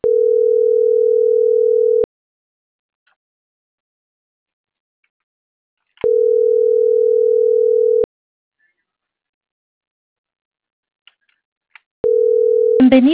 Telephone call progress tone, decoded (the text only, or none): ringback tone